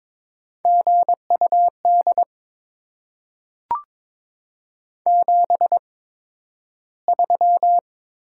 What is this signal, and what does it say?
Morse code, heard as a keyed tone, about 22 wpm, spaced as for 12 wpm, 700 Hz. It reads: GUD 73
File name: Top 20 QSO Elements - 8x Character Spacing - 22wpm